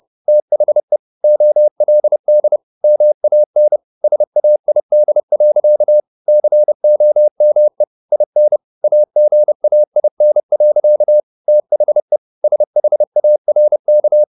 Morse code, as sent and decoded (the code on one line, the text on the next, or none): - .... / --- .-.. -.. / -- .- -. / ... .- .. -.. .-.-.- / -.-. --- -- . / .. -. / .- --. .- .. -. .-.-.- / - .... / ... .... .- .-. -.-
THE OLD MAN SAID. COME IN AGAIN. THE SHARK